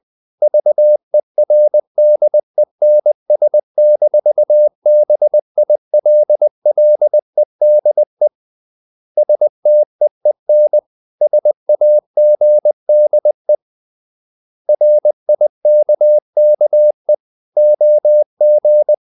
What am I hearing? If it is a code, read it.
Morse code, heard as a keyed tone, about 20 wpm, 600 Hz. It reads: VERDENS-BILLEDE STEEN SAGDE RIKKE OG
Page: Ferie 20 wpm